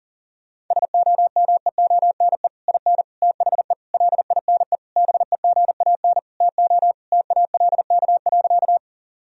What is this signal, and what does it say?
Morse code, heard as a keyed tone, about 40 wpm, 700 Hz. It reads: SOMEONE IN THE LINE BEGAN TO TALK.
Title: Sentences from Top 1000 Words - Review Part 1 - 40wpm